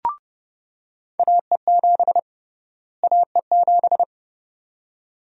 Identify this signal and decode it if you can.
Morse code, heard as a keyed tone, about 30 wpm, 700 Hz. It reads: AE7 AE7
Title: USA Call Sign Prefixes - Repeated 2x - 30wpm